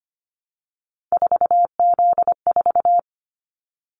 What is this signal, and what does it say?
Morse code, heard as a keyed tone, about 25 wpm, 700 Hz. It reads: 4Z4
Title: Callsign Prefixes for Top 50 DXCC - 25wpm